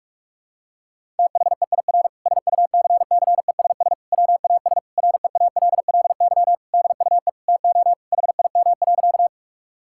Morse code, heard as a keyed tone, about 46 wpm, 700 Hz. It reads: THEIR SUCCESS WAS REALLY DUE TO HIM.
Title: Sentences from Top 1000 Words - Review Part 6 - 45wpm